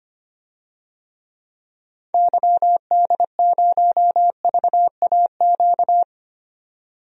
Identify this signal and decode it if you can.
Morse code, heard as a keyed tone, about 25 wpm, 700 Hz. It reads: YD0VAQ
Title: DXCC-100 Call Signs - 6 Characters - 25wpm